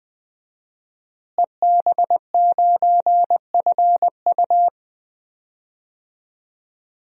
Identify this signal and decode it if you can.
Morse code, heard as a keyed tone, about 20 wpm, 700 Hz. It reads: EB9FU
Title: DXCC-100 Call Signs - 5 Characters - Repeated 3x - 20wpm